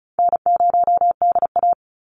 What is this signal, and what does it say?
Morse code, heard as a keyed tone, about 36 wpm, 700 Hz. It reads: N0DA
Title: NA SOTA Chaser Callsigns 35wpm